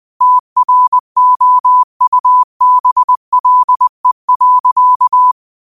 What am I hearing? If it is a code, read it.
Morse code, heard as a keyed tone, about 20 wpm, 1000 Hz. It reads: TROUBLE.
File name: qotdfiles / 20wpm